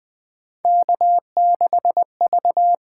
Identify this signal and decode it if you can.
Morse code, heard as a keyed tone, about 20 wpm, 700 Hz. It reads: K6V
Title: USA Call Signs - 8x Character Spacing - 20wpm